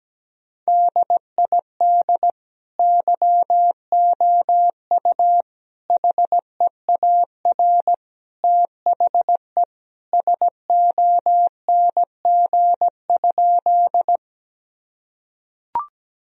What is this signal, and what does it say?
Morse code, heard as a keyed tone, about 17 wpm, 700 Hz. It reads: DID YOU HEAR THE SONG?
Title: Sentences from Top 1000 Words - Review Part 2 - 17wpm